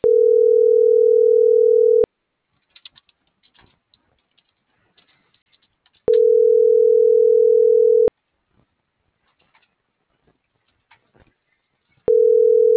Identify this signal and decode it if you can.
Ringback tone (North America), 440 + 480 Hz.